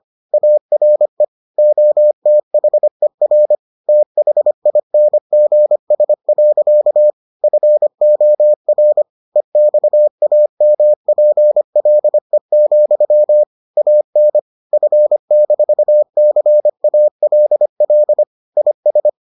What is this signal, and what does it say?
Morse code, heard as a keyed tone, about 25 wpm, 600 Hz. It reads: ARE OTHER THINGS. FOR EXAMPLE, AN F-CALL IS